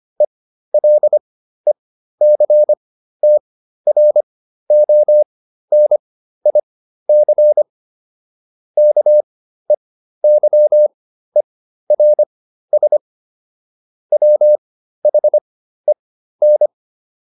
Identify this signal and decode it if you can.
Morse code, heard as a keyed tone, about 25 wpm, spaced as for 13 wpm, 600 Hz. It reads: ELECTRONIC KEYERS WHEN